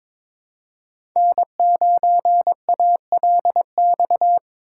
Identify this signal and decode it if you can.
Morse code, heard as a keyed tone, about 22 wpm, 700 Hz. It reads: N9ALX
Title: USA Call Signs - 5 Characters - 3x Character Spacing - 22wpm